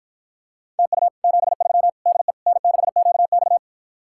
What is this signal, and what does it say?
Morse code, heard as a keyed tone, about 52 wpm, 700 Hz. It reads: TU 73 DE N6QX